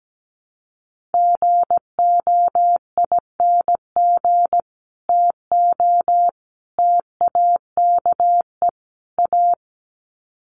Morse code, heard as a keyed tone, about 17 wpm, 700 Hz. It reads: GOING TO TAKE A
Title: Sets of 4 Words 17wpm